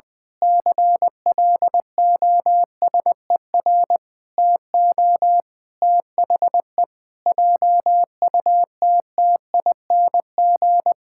Morse code, heard as a keyed tone, about 21 wpm, 700 Hz. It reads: CLOSER TO THE JUTTING
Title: Pangram Challenge v2 - 20wpm